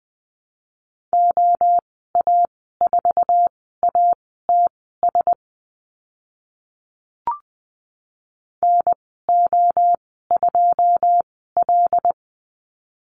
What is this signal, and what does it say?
Morse code, heard as a keyed tone, about 20 wpm, spaced as for 15 wpm, 700 Hz. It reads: OA4ATS NO2L
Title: 082 Lesson 14g - Number 2 - Call Signs 20wpm